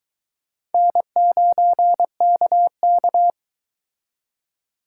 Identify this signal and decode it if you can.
Morse code, heard as a keyed tone, about 22 wpm, 700 Hz. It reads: N9KK